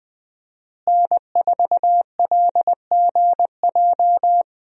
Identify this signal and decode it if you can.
Morse code, heard as a keyed tone, about 21 wpm, 700 Hz. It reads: N4LGJ
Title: USA Call Signs - 5 Characters - Repeated 3x - 20wpm